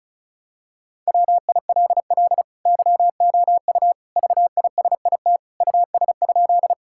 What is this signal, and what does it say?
Morse code, heard as a keyed tone, about 36 wpm, 700 Hz. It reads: WILL YOU VISIT US?
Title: Sentences from Top 1000 Words - Review Part 6 - 35wpm